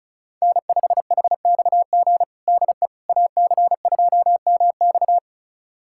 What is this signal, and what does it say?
Morse code, heard as a keyed tone, about 36 wpm, 700 Hz. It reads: N5HXG DE AC2MX
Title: USA Callsign DE Callsign - 5 Characters - 35wpm